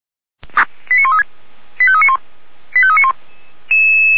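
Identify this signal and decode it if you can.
Five-tone selective call (ZVEI), heard as quick runs of five tones, tones of about 70 ms. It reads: 89136 85391 85391
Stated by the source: Five-tone ZVEI selective call received by SDR.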